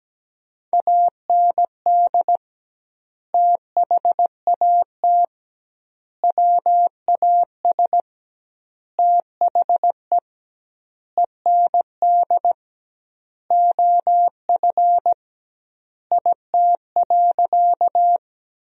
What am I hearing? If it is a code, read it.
Morse code, heard as a keyed tone, about 17 wpm, 700 Hz. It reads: AND THAT WAS THE END OF IT.
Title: 2x Word-Spacing - Sentences from Top 200 Words - 17wpm